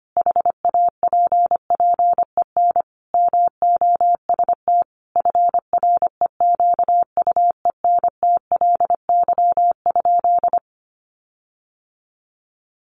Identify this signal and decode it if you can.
Morse code, heard as a keyed tone, about 25 wpm, 700 Hz. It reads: HAPPEN MOST FREQUENTLY?
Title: Sentences from Top 1300 Words 25wpm